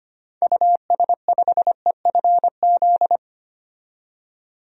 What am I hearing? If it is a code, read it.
Morse code, heard as a keyed tone, about 25 wpm, 700 Hz. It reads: US5EFZ
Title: DXCC-100 Call Signs - 6 Characters - 25wpm